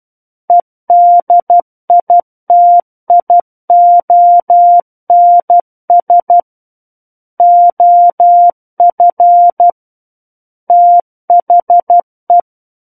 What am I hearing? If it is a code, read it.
Morse code, heard as a keyed tone, about 12 wpm, 700 Hz. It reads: EDITIONS OF THE
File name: Book One / 03-Chapter 3 - 12 WPM